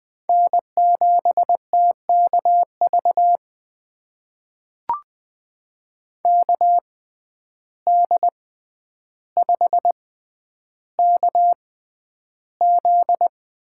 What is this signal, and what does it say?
Morse code, heard as a keyed tone, about 21 wpm, spaced as for 10 wpm, 700 Hz. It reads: N7TKV KD5KZ
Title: USA Call Signs - 5 Characters - 6x Character Spacing - 20wpm